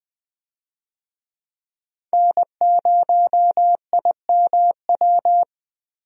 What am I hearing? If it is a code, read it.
Morse code, heard as a keyed tone, about 20 wpm, 700 Hz. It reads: N0IMW
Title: USA Call Signs - 5 Characters - Repeated 2x - 20wpm